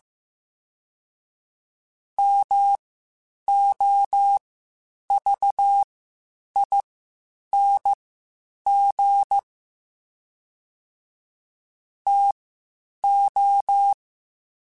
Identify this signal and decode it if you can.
Morse code, heard as a keyed tone, about 15 wpm, spaced as for 8 wpm, 800 Hz. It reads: MOVING TO
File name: QSOs / QSOs-mp3s / WD0BGP 1@10